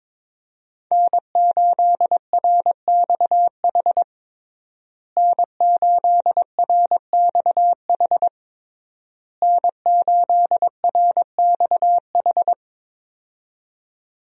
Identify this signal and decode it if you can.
Morse code, heard as a keyed tone, about 22 wpm, 700 Hz. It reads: N8RXH N8RXH N8RXH
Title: USA Call Signs - 5 Characters - Repeated 3x - 22wpm